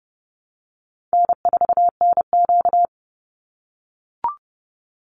Morse code, heard as a keyed tone, about 30 wpm, 700 Hz. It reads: N4NQ
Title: Easy Call Signs - 30wpm